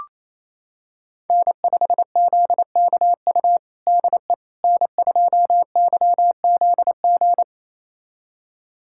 Morse code, heard as a keyed tone, about 28 wpm, 700 Hz. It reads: N5ZKU DE N2YZG
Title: USA Callsign DE Callsign - 5 Characters - 28wpm